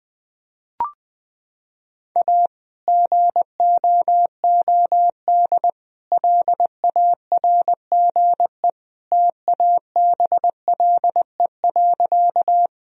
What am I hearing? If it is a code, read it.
Morse code, heard as a keyed tone, about 20 wpm, 700 Hz. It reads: A GOOD LARGE TABLE.
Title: Sentences from Top 1000 Words - Review Part 2 - 20wpm